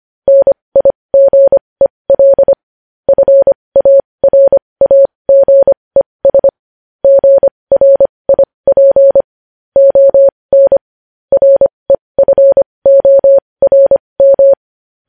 Morse code, heard as a keyed tone, about 25 wpm, spaced as for 20 wpm, 550 Hz. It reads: NIGEL FARAGES GRIP ON REFORM
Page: news-headlines-in-morse-code-at-20-wpm